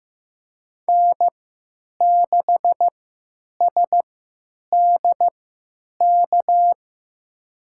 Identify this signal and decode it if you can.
Morse code, heard as a keyed tone, about 15 wpm, spaced as for 9 wpm, 700 Hz. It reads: N6SDK
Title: USA Call Signs - 3x Character Spacing - 15wpm